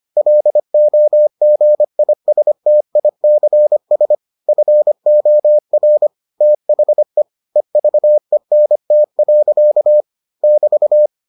Morse code, heard as a keyed tone, about 25 wpm, 600 Hz. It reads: LOGISTICS FOR THE EVENT. <BT>